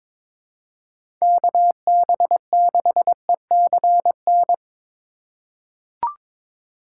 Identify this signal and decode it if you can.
Morse code, heard as a keyed tone, about 22 wpm, 700 Hz. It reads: KB6ECN
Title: USA Call Signs - 6 Characters - 2x Character Spacing - 22wpm